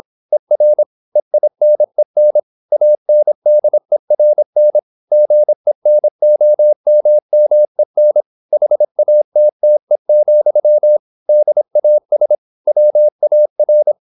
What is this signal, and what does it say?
Morse code, heard as a keyed tone, about 26 wpm, 600 Hz. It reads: ER EINEN ANDERN GENOMMEN HATTE, DAS WAR